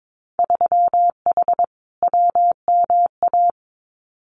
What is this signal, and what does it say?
Morse code, heard as a keyed tone, about 22 wpm, 700 Hz. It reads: 3H WMA
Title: Winter Field Day Exchanges Encore 22wpm